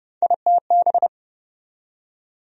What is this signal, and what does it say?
Morse code, heard as a keyed tone, about 31 wpm, 700 Hz. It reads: ITB
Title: USA Call Sign Postfixes - 3 Letters 30wpm